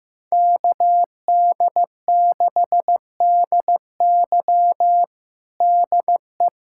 Morse code, heard as a keyed tone, about 16 wpm, 700 Hz. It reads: KD6DY DE
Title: USA Callsign DE Callsign - 5 Characters - 15wpm